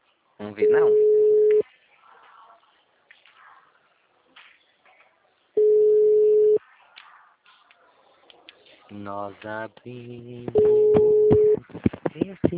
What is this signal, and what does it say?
Ringback tone (Europe, 425 Hz), with speech.